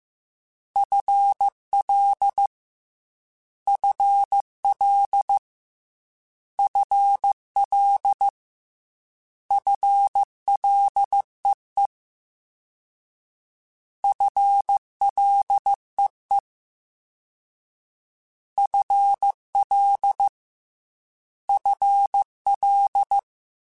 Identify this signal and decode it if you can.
Morse code, heard as a keyed tone, about 15 wpm, 800 Hz. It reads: FL FL FL FLEE FLEE FL FL
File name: FL / FL mp3s / FL Prefixes 3x2 @13